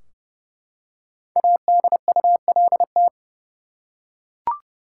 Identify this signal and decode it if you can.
Morse code, heard as a keyed tone, about 32 wpm, 700 Hz. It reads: ADULT
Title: Five Letter Words - 30wpm